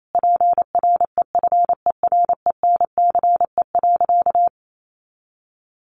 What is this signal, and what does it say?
Morse code, heard as a keyed tone, about 28 wpm, 700 Hz. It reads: PREFERENCE.
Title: Sentences from Top 1500 Words 28wpm